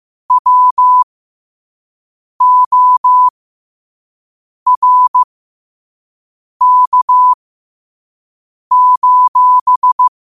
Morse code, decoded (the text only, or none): WORK: